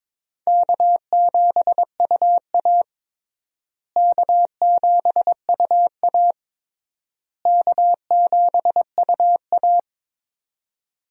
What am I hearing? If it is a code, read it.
Morse code, heard as a keyed tone, about 22 wpm, 700 Hz. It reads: K7UA K7UA K7UA
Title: USA Call Signs - 4 Characters - Repeated 3x - 22wpm